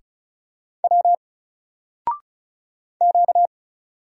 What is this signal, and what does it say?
Morse code, heard as a keyed tone, about 36 wpm, 700 Hz. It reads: W Q
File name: Single-Letter-Number-35wpm